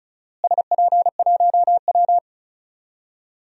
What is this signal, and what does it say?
Morse code, heard as a keyed tone, about 34 wpm, 700 Hz. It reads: SP1W
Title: DXCC-100 Call Signs - 4 Characters - Repeated 3x - 35wpm